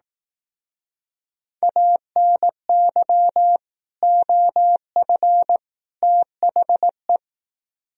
Morse code, heard as a keyed tone, about 18 wpm, 700 Hz. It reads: ANY OF THE
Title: Speed-Racing - Sets of 3 Words 15wpm